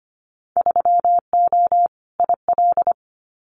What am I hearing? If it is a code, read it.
Morse code, heard as a keyed tone, about 25 wpm, 700 Hz. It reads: 3O IL
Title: Winter Field Day Exchanges Encore 25wpm